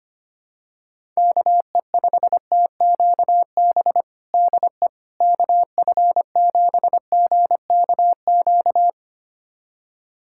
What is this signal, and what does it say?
Morse code, heard as a keyed tone, about 25 wpm, 700 Hz. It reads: KE5TQB DE KF7GKQ
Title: USA Callsign DE Callsign - 6 Characters - 25wpm